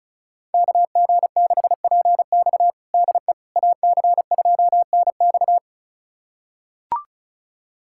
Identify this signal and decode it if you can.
Morse code, heard as a keyed tone, about 36 wpm, 700 Hz. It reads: KG6PX DE AC2NX
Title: USA Callsign DE Callsign - 5 Characters - 35wpm